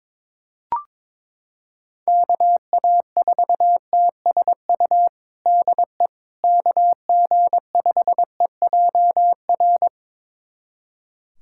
Morse code, heard as a keyed tone, about 22 wpm, 700 Hz. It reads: KA4TSU DE KG5EJR
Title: USA Callsign DE Callsign - 6 Characters - 22wpm